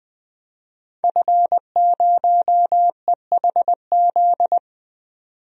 Morse code, heard as a keyed tone, about 20 wpm, 700 Hz. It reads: F0EHZ
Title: DXCC-100 Call Signs - 5 Characters - 20wpm